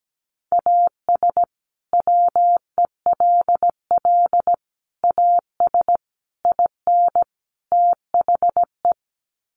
Morse code, heard as a keyed tone, about 17 wpm, 700 Hz. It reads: AS WELL AS IN THE
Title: Sets of 5 Words Encore 17wpm